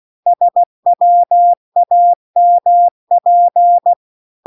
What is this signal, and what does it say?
Morse code, heard as a keyed tone, about 16 wpm, 700 Hz. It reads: SWAMP